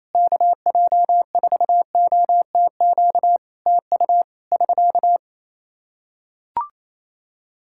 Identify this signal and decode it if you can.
Morse code, heard as a keyed tone, about 28 wpm, 700 Hz. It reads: KJ4OTQ TU <SK>
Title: Practice QSO - Exchange 4 - Ending - 28wpm